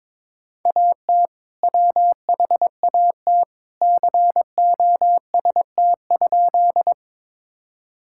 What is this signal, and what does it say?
Morse code, heard as a keyed tone, about 22 wpm, 700 Hz. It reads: AT WHAT COST?
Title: Sentences from Top 1000 Words - Review Part 5 - 22wpm